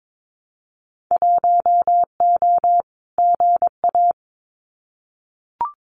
Morse code, heard as a keyed tone, about 22 wpm, 700 Hz. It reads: 1O GA
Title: Winter Field Day Exchanges 22wpm